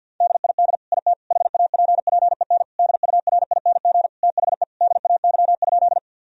Morse code, heard as a keyed tone, about 50 wpm, 700 Hz. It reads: DID IT HAPPEN DURING THE DAY?
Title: Sentences from Top 1000 Words - Review Part 2 - 50wpm